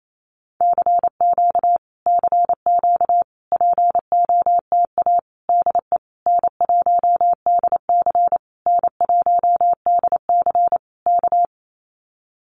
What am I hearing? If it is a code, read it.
Morse code, heard as a keyed tone, about 28 wpm, 700 Hz. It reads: CQ CQ POTA DE N1DC N1DC K
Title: Common QSO Phrases Encore 28wpm